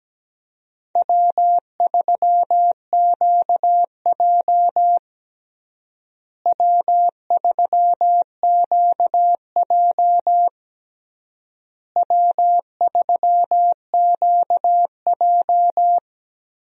Morse code, heard as a keyed tone, about 17 wpm, 700 Hz. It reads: W3QJ W3QJ W3QJ
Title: USA Call Signs - 4 Characters - Repeated 3x - 17wpm